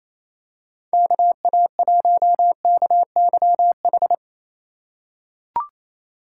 Morse code, heard as a keyed tone, about 28 wpm, 700 Hz. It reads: KA1KYH
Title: USA Call Signs - 3x Character Spacing - 28wpm